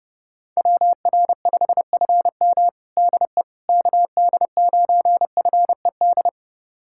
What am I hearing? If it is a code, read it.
Morse code, heard as a keyed tone, about 30 wpm, 700 Hz. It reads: WR5FM DE KD9FED
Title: USA Callsign DE Callsign 30wpm